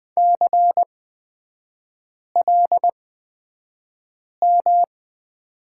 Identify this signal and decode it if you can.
Morse code, heard as a keyed tone, about 20 wpm, spaced as for 5 wpm, 700 Hz. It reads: CLM
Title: 243 Lesson 39e - Letter z - Three Characters 20wpm